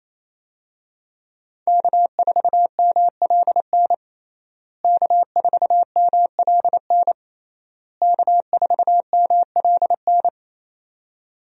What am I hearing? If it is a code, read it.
Morse code, heard as a keyed tone, about 28 wpm, 700 Hz. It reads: K4MLN K4MLN K4MLN
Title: USA Call Signs - 5 Characters - Repeated 3x - 28wpm